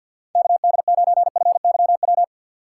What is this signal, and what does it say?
Morse code, heard as a keyed tone, about 48 wpm, 700 Hz. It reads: KD9FYW
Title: USA Call Signs - 6 Characters - 6x Character Spacing - 50wpm